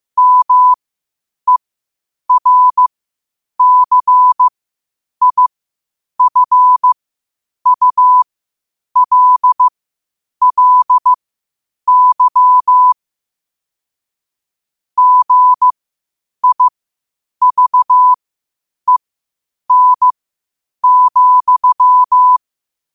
Morse code, heard as a keyed tone, about 15 wpm, spaced as for 8 wpm, 1000 Hz. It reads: MERCIFULLY GIVEN,
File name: qotdfiles / 8wpm